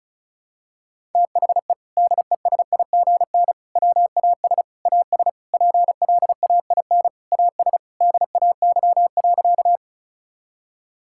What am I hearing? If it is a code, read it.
Morse code, heard as a keyed tone, about 36 wpm, 700 Hz. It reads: THE DESIGN WAS AS PLAIN AS DAY.
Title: Sentences from Top 1000 Words - Review Part 5 - 35wpm